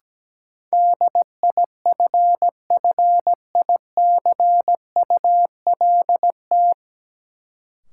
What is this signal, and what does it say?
Morse code, heard as a keyed tone, about 17 wpm, 700 Hz. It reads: DIFFICULT
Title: Sets of 4 Words - New Word Drill - 17wpm